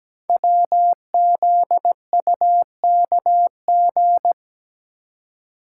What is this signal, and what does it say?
Morse code, heard as a keyed tone, about 17 wpm, 700 Hz. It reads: WZUKG
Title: Code Groups - 5 Letters - 17wpm